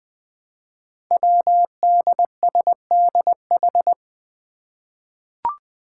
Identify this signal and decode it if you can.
Morse code, heard as a keyed tone, about 20 wpm, 700 Hz. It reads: WDSDH
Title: Five Challenging Letter-Number Combinations 20wpm